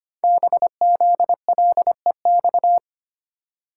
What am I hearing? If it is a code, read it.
Morse code, heard as a keyed tone, about 25 wpm, 700 Hz. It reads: BZLEX
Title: Code Groups - 5 Letters - 25wpm